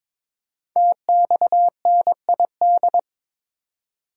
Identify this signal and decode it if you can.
Morse code, heard as a keyed tone, about 22 wpm, 700 Hz. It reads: TXNID
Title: Code Groups - 5 Letter:Numbers - 22wpm